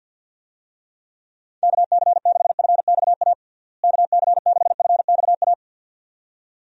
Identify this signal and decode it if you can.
Morse code, heard as a keyed tone, about 53 wpm, 700 Hz. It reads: KC6FXA KC6FXA
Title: USA Call Signs - 6 Characters - Repeated 2x - 50wpm